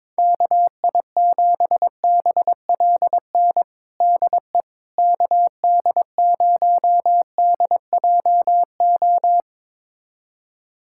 Morse code, heard as a keyed tone, about 22 wpm, 700 Hz. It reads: KI7BLN DE KD0DJO
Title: USA Callsign DE Callsign 22wpm